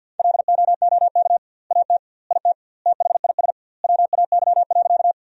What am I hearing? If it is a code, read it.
Morse code, heard as a keyed tone, about 50 wpm, 700 Hz. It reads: LOOK AT IT THIS WAY.
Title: 2x Word-Spacing - Sentences from Top 100 Words - 50wpm